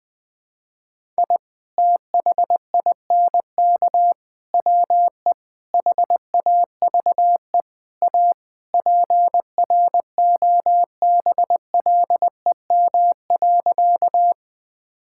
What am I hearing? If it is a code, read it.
Morse code, heard as a keyed tone, about 20 wpm, 700 Hz. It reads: I THINK WE HAVE A PROBLEM.
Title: Sentences with 4-Word Phrases - 20wpm